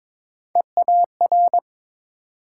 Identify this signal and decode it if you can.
Morse code, heard as a keyed tone, about 31 wpm, spaced as for 27 wpm, 700 Hz. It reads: EAR
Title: Speed-Racing Multi-letter Phonemes - 22wpm